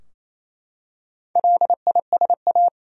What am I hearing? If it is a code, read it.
Morse code, heard as a keyed tone, about 28 wpm, 700 Hz. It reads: LISA
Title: USA Common Names - Max 5 Letters - 28wpm